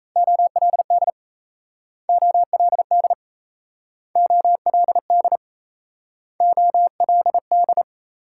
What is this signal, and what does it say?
Morse code, heard as a keyed tone, about 37 wpm, 700 Hz. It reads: OLD OLD OLD OLD